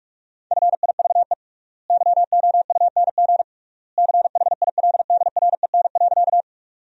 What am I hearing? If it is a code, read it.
Morse code, heard as a keyed tone, about 46 wpm, 700 Hz. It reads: FIVE YOUNG CHILDREN.
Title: 3x Word-Spacing - Sentences from Top 400 Words - 45wpm